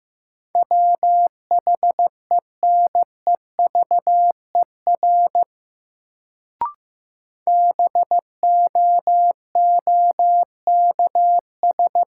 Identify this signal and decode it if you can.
Morse code, heard as a keyed tone, about 15 wpm, 700 Hz. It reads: WHENEVER BOOKS
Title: Sentences from Top 1200 Words - New Word Drill Alt - 15wpm